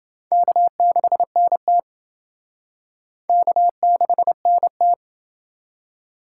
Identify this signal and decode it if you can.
Morse code, heard as a keyed tone, about 29 wpm, 700 Hz. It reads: K6NT K6NT